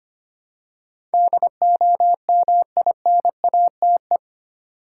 Morse code, heard as a keyed tone, about 25 wpm, 700 Hz. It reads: DOMINATE
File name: 8 Letter Words - First 3 Letters Given - 25wpm